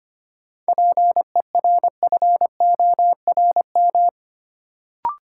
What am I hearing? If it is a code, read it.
Morse code, heard as a keyed tone, about 25 wpm, 700 Hz. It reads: PERFORM
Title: Sentences from Top 1000 Words - Review Part 5 - New Word Drill - 25wpm